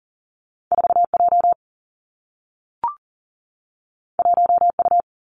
Morse code, heard as a keyed tone, about 40 wpm, 700 Hz. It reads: 4 J 1 U
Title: Two Challenging Letter-Number Combinations 40wpm